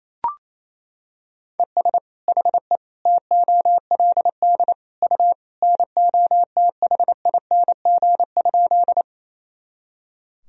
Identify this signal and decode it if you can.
Morse code, heard as a keyed tone, about 28 wpm, 700 Hz. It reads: ES HE TOLD U NOTHING?
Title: Sentences from Top 500 Words and Abbreviations 28wpm